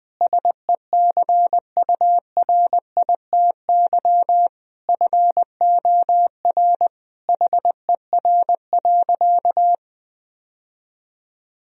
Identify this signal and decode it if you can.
Morse code, heard as a keyed tone, about 20 wpm, 700 Hz. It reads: SECURITY FOR HER.
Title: Sentences from Top 1000 Words - Review Part 4 - 20wpm